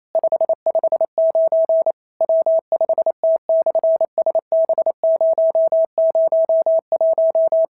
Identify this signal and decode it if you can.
Morse code, heard as a keyed tone, about 28 wpm, 650 Hz. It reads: 559 W5T/SB001
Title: SOTA QSO Practice 28wpm